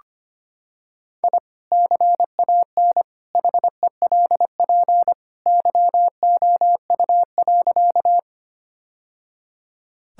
Morse code, heard as a keyed tone, about 25 wpm, 700 Hz. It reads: I CAN HELP YOU.
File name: practice / 25wpm / Sentences with 2-Word Phrases - 25wpm